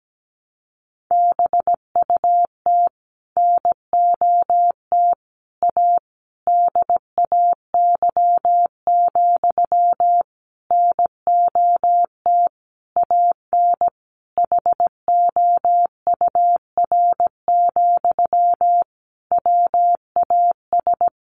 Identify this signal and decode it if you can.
Morse code, heard as a keyed tone, about 17 wpm, 700 Hz. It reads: BUT NOT A DAY, NOT AN HOUR, WAS
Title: Sentences from Top 1400 Words 17wpm